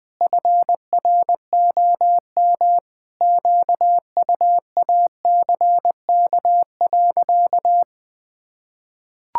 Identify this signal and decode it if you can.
Morse code, heard as a keyed tone, about 21 wpm, 700 Hz. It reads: FROM QUACK.
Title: Pangram Challenge v2 - 20wpm